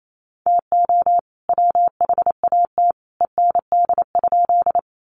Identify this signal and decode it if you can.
Morse code, heard as a keyed tone, about 28 wpm, 700 Hz. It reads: TO WHAT END?
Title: Sentences from Top 200 Words 28wpm